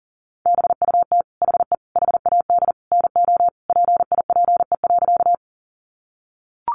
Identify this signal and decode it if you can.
Morse code, heard as a keyed tone, about 40 wpm, 700 Hz. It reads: BUT HE HAD NO PIPE.
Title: Sentences from Top 1500 Words 40wpm